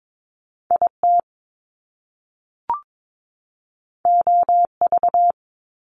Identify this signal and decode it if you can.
Morse code, heard as a keyed tone, about 22 wpm, 700 Hz. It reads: IT OV
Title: Top Two Letter Combinations in Words 22wpm